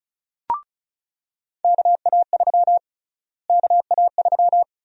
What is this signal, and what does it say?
Morse code, heard as a keyed tone, about 36 wpm, 700 Hz. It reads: KA3 KA3
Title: USA Call Sign Prefixes - Repeated 2x - 35wpm